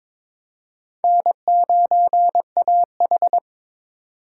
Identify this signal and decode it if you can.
Morse code, heard as a keyed tone, about 22 wpm, 700 Hz. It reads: N9AH
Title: USA Call Signs - 4 Characters - 3x Character Spacing - 22wpm